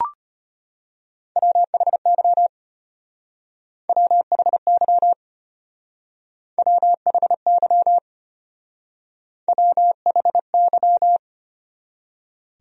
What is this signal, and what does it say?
Morse code, heard as a keyed tone, about 32 wpm, 700 Hz. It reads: WHY WHY WHY WHY